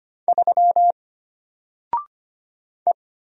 Morse code, heard as a keyed tone, about 25 wpm, 700 Hz. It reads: 3 E
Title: Single Letter-Number 25wpm